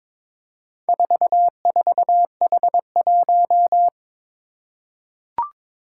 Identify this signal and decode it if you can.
Morse code, heard as a keyed tone, about 22 wpm, 700 Hz. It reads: 44H1
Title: Four Challenging Letter-Number Combinations 22wpm